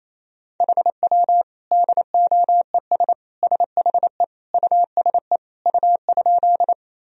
Morse code, heard as a keyed tone, about 28 wpm, 700 Hz. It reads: HW DOES SHE USE U?
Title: Sentences from Top 200 Words and Abbreviations 28wpm